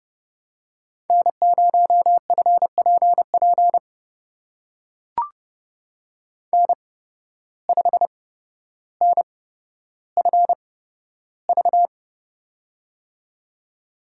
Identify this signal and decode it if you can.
Morse code, heard as a keyed tone, about 30 wpm, spaced as for 13 wpm, 700 Hz. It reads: N0FPP N5NFV
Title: USA Call Signs - 5 Characters - 8x Character Spacing - 30wpm